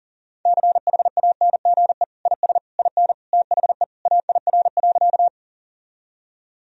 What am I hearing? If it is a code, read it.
Morse code, heard as a keyed tone, about 40 wpm, 700 Hz. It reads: CHANGE IS IN THE AIR.
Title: Sentences from Top 1000 Words - Review Part 1 - 40wpm